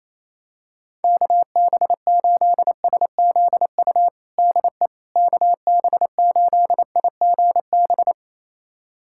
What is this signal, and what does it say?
Morse code, heard as a keyed tone, about 27 wpm, 700 Hz. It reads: KB8SZU DE KB8IGB
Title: USA Callsign DE Callsign - 6 Characters - 28wpm